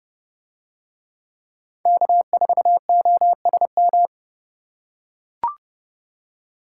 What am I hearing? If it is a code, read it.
Morse code, heard as a keyed tone, about 29 wpm, 700 Hz. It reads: K4OSM